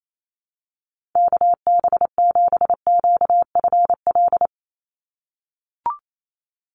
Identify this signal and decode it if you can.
Morse code, heard as a keyed tone, about 28 wpm, 700 Hz. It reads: KB7QFL
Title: USA Call Signs 28wpm